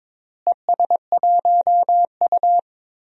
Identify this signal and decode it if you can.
Morse code, heard as a keyed tone, about 22 wpm, 700 Hz. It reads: ES1U
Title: DXCC-100 Call Signs - 4 Characters - Repeated 3x - 22wpm